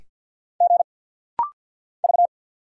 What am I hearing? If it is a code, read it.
Morse code, heard as a keyed tone, about 50 wpm, 700 Hz. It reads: G V
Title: Single Letters - Rapid-Fire 50wpm